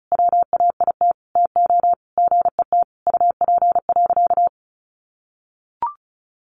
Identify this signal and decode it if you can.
Morse code, heard as a keyed tone, about 34 wpm, 700 Hz. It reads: WAIT TO GET UP.
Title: Sentences from Top 500 Words 35wpm